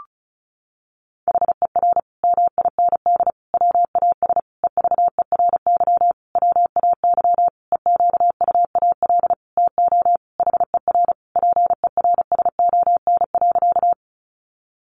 Morse code, heard as a keyed tone, about 34 wpm, 700 Hz. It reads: HER MIND WAS EVERY WAY EQUAL TO HER PERSON.
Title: Sentences from Top 1400 Words 35wpm